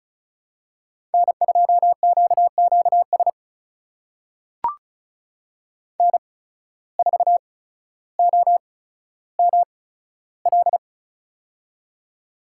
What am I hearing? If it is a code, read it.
Morse code, heard as a keyed tone, about 36 wpm, spaced as for 15 wpm, 700 Hz. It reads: N2QQS N4OML
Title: USA Call Signs - 5 Characters - 8x Character Spacing - 35wpm